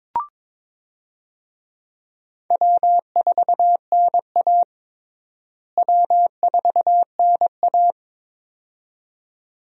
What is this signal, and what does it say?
Morse code, heard as a keyed tone, about 22 wpm, 700 Hz. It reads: W4NA W4NA
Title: USA Call Signs - 4 Characters - Repeated 2x - 22wpm